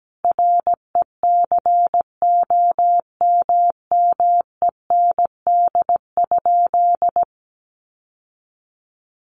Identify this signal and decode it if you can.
Morse code, heard as a keyed tone, about 17 wpm, 700 Hz. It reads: RECOMMEND?
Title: Sentences from Top 1200 Words 17wpm